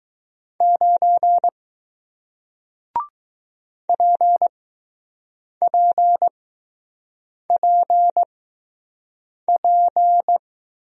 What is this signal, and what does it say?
Morse code, heard as a keyed tone, about 20 wpm, spaced as for 6 wpm, 700 Hz. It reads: 9 PPPP